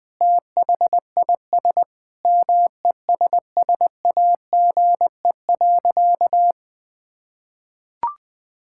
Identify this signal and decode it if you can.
Morse code, heard as a keyed tone, about 20 wpm, 700 Hz. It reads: THIS MESSAGE.
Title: Sentences with 9-Word Phrases - Spoken Beginning - 20wpm